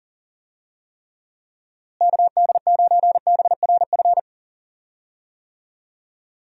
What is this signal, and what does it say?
Morse code, heard as a keyed tone, about 38 wpm, 700 Hz. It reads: KD9BRF
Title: USA Call Signs - 6 Characters - Repeated 2x - 40wpm